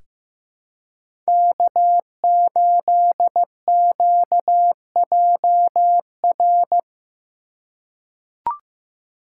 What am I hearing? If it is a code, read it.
Morse code, heard as a keyed tone, about 15 wpm, 700 Hz. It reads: K8QJR
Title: USA Call Signs - 5 Characters - 2x Character Spacing - 15wpm